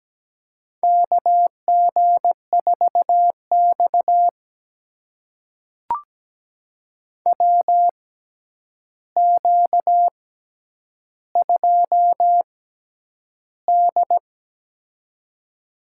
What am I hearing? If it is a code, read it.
Morse code, heard as a keyed tone, about 17 wpm, spaced as for 9 wpm, 700 Hz. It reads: KG4X WQ2D
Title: USA Call Signs - 4 Characters - 6x Character Spacing - 17wpm